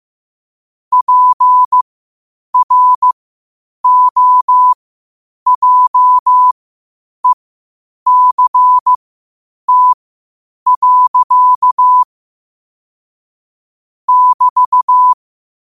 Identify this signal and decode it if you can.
Morse code, heard as a keyed tone, about 15 wpm, spaced as for 8 wpm, 1000 Hz. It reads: PROJECT. <BT>
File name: qotdfiles / 8wpm